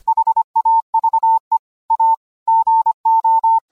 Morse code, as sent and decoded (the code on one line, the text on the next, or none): .... .- ...- . / .- / --. ---
HAVE A GO